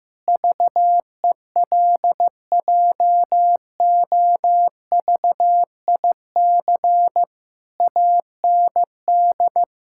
Morse code, heard as a keyed tone, about 16 wpm, 700 Hz. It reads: VELJOVIC AND
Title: Pangram Challenge v2 - 15wpm